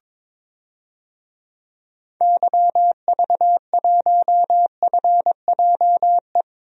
Morse code, heard as a keyed tone, about 22 wpm, 700 Hz. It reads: YV1FJE
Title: DXCC-100 Call Signs - 6 Characters - Repeated 3x - 22wpm